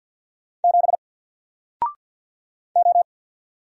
Morse code, heard as a keyed tone, about 49 wpm, 700 Hz. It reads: 7 O
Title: Single Character (Letter:Number:Punctuation:Prosign) Alt 50wpm